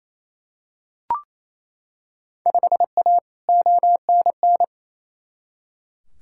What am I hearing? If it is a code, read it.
Morse code, heard as a keyed tone, about 27 wpm, 700 Hz. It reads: 5A ONN
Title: Field Day Exchanges Encore - 28wpm